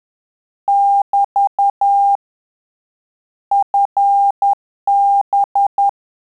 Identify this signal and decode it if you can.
Morse code, heard as a keyed tone, about 10 wpm, 800 Hz. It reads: <BT> FB
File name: Telegraphy-group / QSO CW 10 PPM